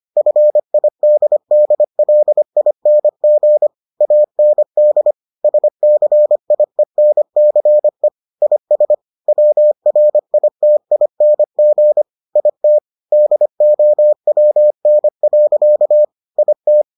Morse code, heard as a keyed tone, about 25 wpm, 600 Hz. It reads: FIDDLING AND SCIENCE IS WRITING IT DOWN. IT